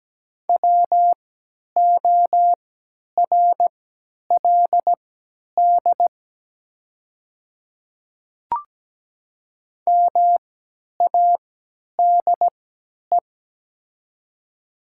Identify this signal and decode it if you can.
Morse code, heard as a keyed tone, about 17 wpm, spaced as for 10 wpm, 700 Hz. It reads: WORLD MADE
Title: 3x Character Spacing - Top 200 Words - 17wpm